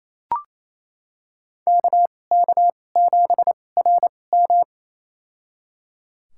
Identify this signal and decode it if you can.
Morse code, heard as a keyed tone, about 28 wpm, spaced as for 20 wpm, 700 Hz. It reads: KK7RM
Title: USA Call Signs - 5 Characters - 2x Character Spacing - 28wpm